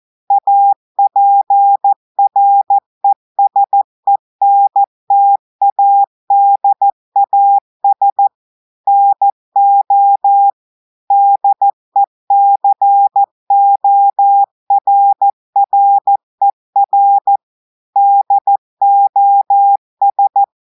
Morse code, heard as a keyed tone, about 14 wpm, 800 Hz. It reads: APRESENTADAS NO DECORRER DOS
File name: audios / portugais / portugais texte en clair 2 vitesse 14 mots minutes